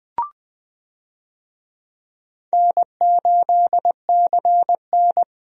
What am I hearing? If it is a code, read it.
Morse code, heard as a keyed tone, about 21 wpm, 700 Hz. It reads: N8CN
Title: USA Call Signs - 4 Characters - Repeated 2x - 20wpm